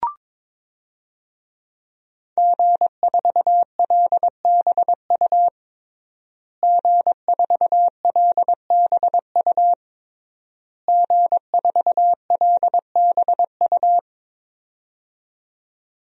Morse code, heard as a keyed tone, about 22 wpm, 700 Hz. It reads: G4LBU G4LBU G4LBU
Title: DXCC-100 Call Signs - 5 Characters - Repeated 3x - 22wpm